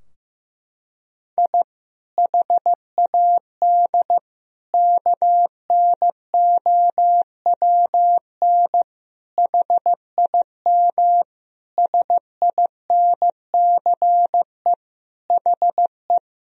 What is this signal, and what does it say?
Morse code, heard as a keyed tone, about 15 wpm, 700 Hz. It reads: I HAD KNOWN HIM SINCE HE
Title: Sentences from Top 1000 Words - Review Part 3 - 15wpm